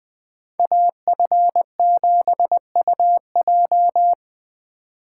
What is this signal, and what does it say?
Morse code, heard as a keyed tone, about 20 wpm, 700 Hz. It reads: AF7UJ